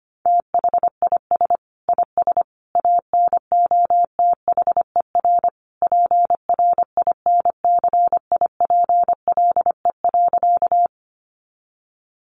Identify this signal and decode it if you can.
Morse code, heard as a keyed tone, about 25 wpm, 700 Hz. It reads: THIS IS ANOTHER PRINCIPLE.
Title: Sentences from Top 1400 Words 25wpm